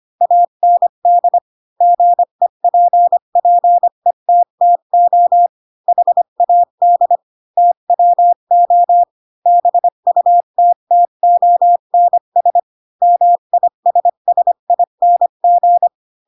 Morse code, heard as a keyed tone, about 25 wpm, 700 Hz. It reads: AND GEPPETTO HAD TWO BUTTONS MISSING